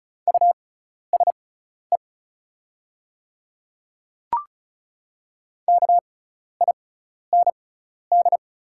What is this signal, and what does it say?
Morse code, heard as a keyed tone, about 36 wpm, spaced as for 12 wpm, 690 Hz. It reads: USE KIND
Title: 6x Character Spacing - Top 500 Words - 35wpm